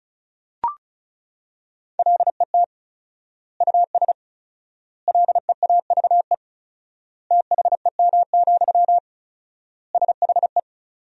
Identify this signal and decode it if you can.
Morse code, heard as a keyed tone, about 36 wpm, 700 Hz. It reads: LET US LEAVE THEM, SHE
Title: 4x Word-Spacing - Sentences from Top 400 Words - 35wpm